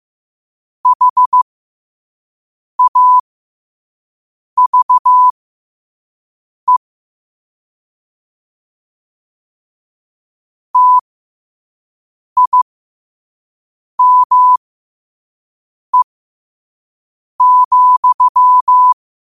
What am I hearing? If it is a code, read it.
Morse code, heard as a keyed tone, about 15 wpm, spaced as for 5 wpm, 1000 Hz. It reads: HAVE TIME,
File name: qotdfiles / 5wpm